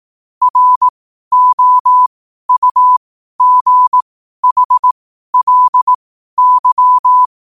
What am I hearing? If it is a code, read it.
Morse code, heard as a keyed tone, about 18 wpm, spaced as for 12 wpm, 1000 Hz. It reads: ROUGHLY
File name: qotdfiles / 12wpm